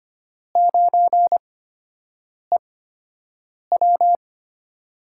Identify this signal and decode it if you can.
Morse code, heard as a keyed tone, about 25 wpm, spaced as for 7 wpm, 700 Hz. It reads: 9EW
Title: USA Call Signs - 8x Character Spacing - 25wpm